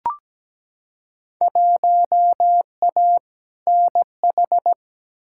Morse code, heard as a keyed tone, about 17 wpm, 700 Hz. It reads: 1A NH
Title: Field Day Exchanges Encore - 17wpm